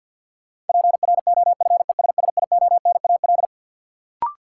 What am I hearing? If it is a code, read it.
Morse code, heard as a keyed tone, about 51 wpm, 700 Hz. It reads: PROFESSIONAL
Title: Sentences from Top 1000 Words - Review Part 4 - New Word Drill - 50wpm